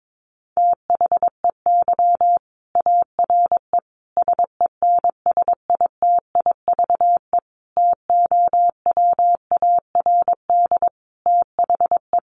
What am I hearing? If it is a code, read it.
Morse code, heard as a keyed tone, about 22 wpm, 700 Hz. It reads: THEY ARE SENSITIVE TOWARD THE
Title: Sentences from Top 1400 Words 22wpm